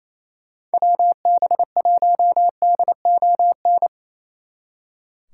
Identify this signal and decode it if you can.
Morse code, heard as a keyed tone, about 34 wpm, 700 Hz. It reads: WB1DON
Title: Speed Racing - USA 2x3 Call Signs 28wpm